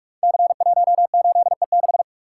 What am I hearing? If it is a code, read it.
Morse code, heard as a keyed tone, about 44 wpm, 700 Hz. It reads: C18E6
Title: Code Groups - 5 Letter:Numbers - 45wpm